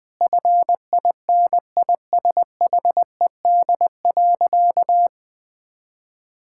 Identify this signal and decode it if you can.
Morse code, heard as a keyed tone, about 20 wpm, 700 Hz. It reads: FINISHED.
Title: Sentences with 3-Word Phrases - 20wpm